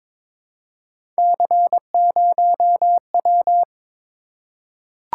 Morse code, heard as a keyed tone, about 23 wpm, 700 Hz. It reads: C0W
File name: practice / 22wpm / Three Letter-Number Combinations - 22wpm